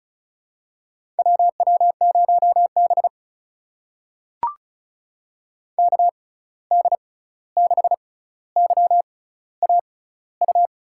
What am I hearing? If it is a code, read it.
Morse code, heard as a keyed tone, about 36 wpm, spaced as for 16 wpm, 700 Hz. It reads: WW0B KD6YAU
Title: USA Call Signs - 6x Character Spacing - 35wpm